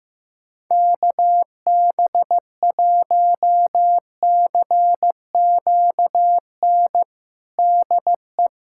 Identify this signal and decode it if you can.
Morse code, heard as a keyed tone, about 16 wpm, 700 Hz. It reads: KB1CQN DE
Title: USA Callsign DE Callsign 15wpm